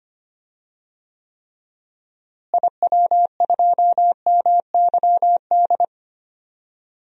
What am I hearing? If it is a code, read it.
Morse code, heard as a keyed tone, about 25 wpm, 700 Hz. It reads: IW2MYD
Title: DXCC-100 Call Signs - 6 Characters - Repeated 2x - 25wpm